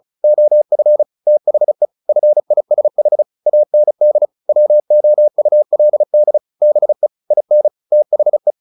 Morse code, heard as a keyed tone, about 34 wpm, 600 Hz. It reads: OF THE FISH AND WOULD BE IN THE
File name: the old man and the sea / the old man and the sea 1 35wpm